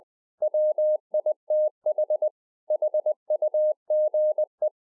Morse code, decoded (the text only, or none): WITH HUGE